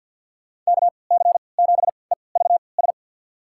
Morse code, heard as a keyed tone, about 50 wpm, spaced as for 28 wpm, 700 Hz. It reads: KC7EVS
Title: USA Call Signs - 6 Characters - 3x Character Spacing - 50wpm